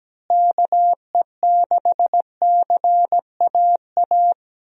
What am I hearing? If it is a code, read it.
Morse code, heard as a keyed tone, about 17 wpm, 700 Hz. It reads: KE6CAA
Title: USA Call Signs - 6 Characters - Repeated 3x - 17wpm